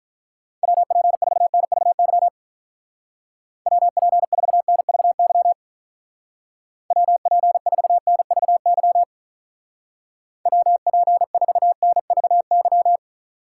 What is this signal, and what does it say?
Morse code, heard as a keyed tone, about 44 wpm, 700 Hz. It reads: WP4NVY WP4NVY WP4NVY WP4NVY